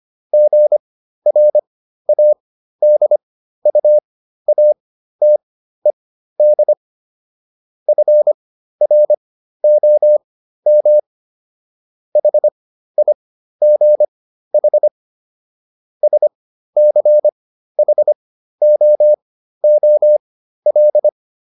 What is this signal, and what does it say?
Morse code, heard as a keyed tone, about 25 wpm, spaced as for 13 wpm, 600 Hz. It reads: GRADUATED FROM HIGH SCHOOL